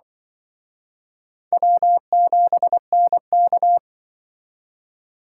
Morse code, heard as a keyed tone, about 25 wpm, 700 Hz. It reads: W7NK
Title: Speed Racing - USA Call Signs 20wpm